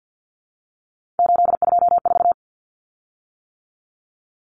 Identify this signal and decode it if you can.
Morse code, heard as a keyed tone, about 48 wpm, 700 Hz. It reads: 824
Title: Common Cut Numbers - 3-Digit 50wpm